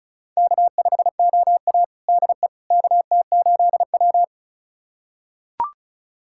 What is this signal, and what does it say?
Morse code, heard as a keyed tone, about 36 wpm, 700 Hz. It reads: K5OA DE KT8W
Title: USA Callsign DE Callsign - 4 Characters - 35wpm